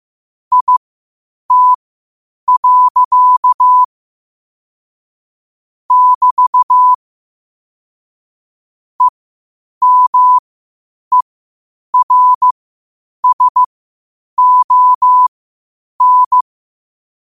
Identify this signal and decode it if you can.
Morse code, heard as a keyed tone, about 15 wpm, spaced as for 8 wpm, 1000 Hz. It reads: IT. <BT> EMERSON
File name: qotdfiles / 8wpm